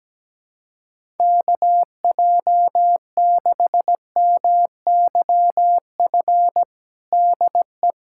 Morse code, heard as a keyed tone, about 17 wpm, 700 Hz. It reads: KJ6MYF DE
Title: USA Callsign DE Callsign 17wpm